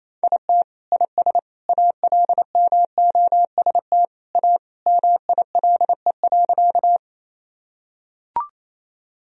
Morse code, heard as a keyed tone, about 28 wpm, 700 Hz. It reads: IT IS ALMOST A MILE.
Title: Sentences from Top 1000 Words - Review Part 4 - 28wpm